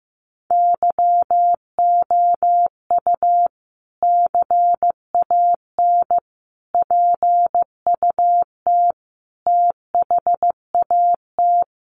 Morse code, heard as a keyed tone, about 15 wpm, 700 Hz. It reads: YOU CAN PUT THAT
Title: Sentences from Top 1500 Words 15wpm